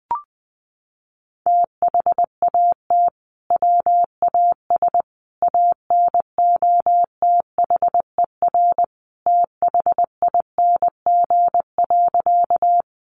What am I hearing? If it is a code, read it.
Morse code, heard as a keyed tone, about 19 wpm, 700 Hz. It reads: THAT WAS ANOTHER THING.
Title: Sentences from Top 600 Words 20wpm